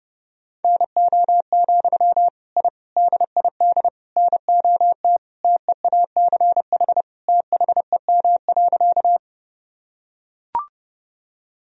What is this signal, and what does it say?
Morse code, heard as a keyed tone, about 30 wpm, 700 Hz. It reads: NO, I DID NOT TEACH THEM.
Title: Sentences from Top 1000 Words - Review Part 2 - 30wpm